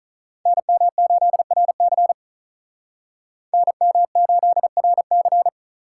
Morse code, heard as a keyed tone, about 45 wpm, 700 Hz. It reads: NM8RC NM8RC